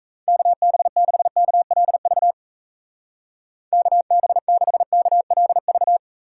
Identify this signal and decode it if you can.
Morse code, heard as a keyed tone, about 37 wpm, 700 Hz. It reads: KB6KLV KB6KLV